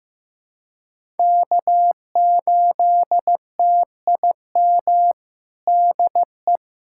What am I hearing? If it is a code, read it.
Morse code, heard as a keyed tone, about 15 wpm, 700 Hz. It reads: K8TIM DE
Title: USA Callsign DE Callsign - 5 Characters - 15wpm